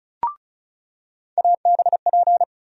Morse code, heard as a keyed tone, about 36 wpm, 700 Hz. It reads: ABP
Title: USA Call Sign Postfixes - 3 Letters 35wpm